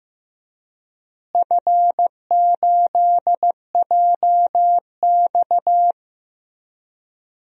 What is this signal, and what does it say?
Morse code, heard as a keyed tone, about 15 wpm, 700 Hz. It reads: F8JX
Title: DXCC-100 Call Signs - 4 Characters - Repeated 3x - 15wpm